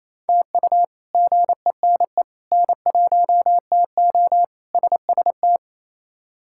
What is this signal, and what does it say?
Morse code, heard as a keyed tone, about 28 wpm, 700 Hz. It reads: TU GENE N1TO SST
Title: K1USN SST QSO Practice - 28wpm